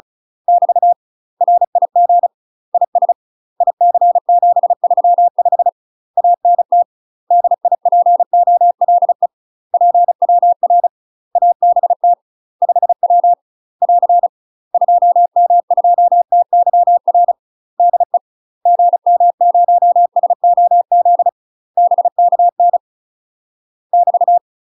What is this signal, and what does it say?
Morse code, heard as a keyed tone, about 34 wpm, 700 Hz. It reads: <BT> RIG IS IC735 ANT DIPOLE PWR ABT 5W <AR> 2M2TYR DE GM0SOZ BKN <BT>
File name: UK MASTER QSOs TRAINING / 35WPM / UK MASTER QSOs 0001